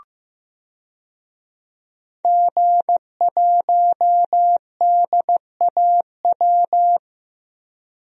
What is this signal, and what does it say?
Morse code, heard as a keyed tone, about 15 wpm, 700 Hz. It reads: G1DAW
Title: DXCC-100 Call Signs - 5 Characters - 15wpm